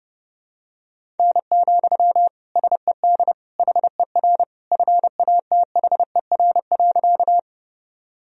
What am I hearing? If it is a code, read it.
Morse code, heard as a keyed tone, about 30 wpm, 700 Hz. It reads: N, SED HER FATHER.
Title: Sentences from Top 300 Words and Abbreviations 30wpm